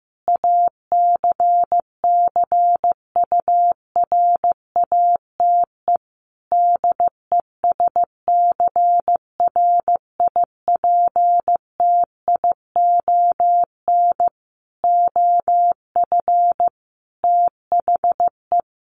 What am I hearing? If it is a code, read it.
Morse code, heard as a keyed tone, about 15 wpm, 700 Hz. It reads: ACCURATE DESCRIPTION OF THE
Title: Sentences from Top 1300 Words 15wpm